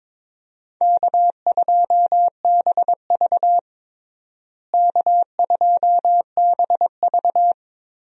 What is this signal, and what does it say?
Morse code, heard as a keyed tone, about 22 wpm, 700 Hz. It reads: K2BV K2BV
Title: USA Call Signs - 4 Characters - Repeated 2x - 22wpm